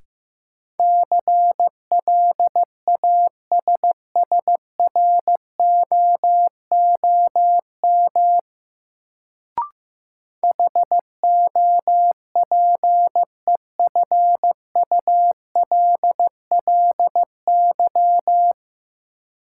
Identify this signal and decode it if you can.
Morse code, heard as a keyed tone, about 15 wpm, 700 Hz. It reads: CLASSROOM HOPEFULLY
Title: Sentences from Top 1500 Words - New Word Drill - 15wpm